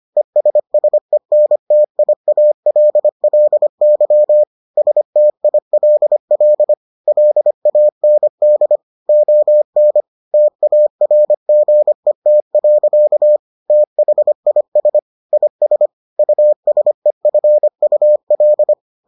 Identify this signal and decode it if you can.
Morse code, heard as a keyed tone, about 25 wpm, 600 Hz. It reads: ESSENTIALLY STILL LAND ON TARGET. THIS IS USEFUL